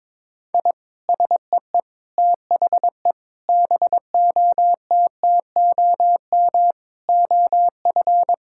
Morse code, heard as a keyed tone, about 22 wpm, 700 Hz. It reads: I SEE THE BOTTOM OF
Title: Sentences from Top 1000 Words - Review Part 6 - 22wpm